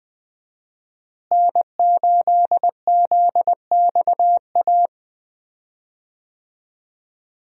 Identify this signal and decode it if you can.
Morse code, heard as a keyed tone, about 20 wpm, 700 Hz. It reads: N8ZXA
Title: USA Call Signs - 5 Characters - Repeated 3x - 20wpm